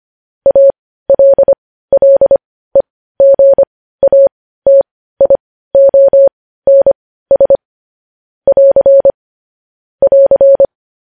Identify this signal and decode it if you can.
Morse code, heard as a keyed tone, about 25 wpm, spaced as for 15 wpm, 550 Hz. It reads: ALLEGATIONS <AR> <AR>
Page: news headlines in morse code at 15 wpm